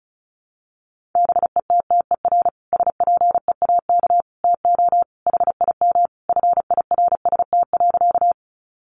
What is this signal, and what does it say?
Morse code, heard as a keyed tone, about 34 wpm, 700 Hz. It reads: BETTER SPEAK TO HIM FIRST.
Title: Sentences from Top 1100 Words 35wpm